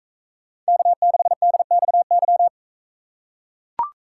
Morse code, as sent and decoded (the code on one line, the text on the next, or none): -.- -.... -.. -..- -.--
K6DXY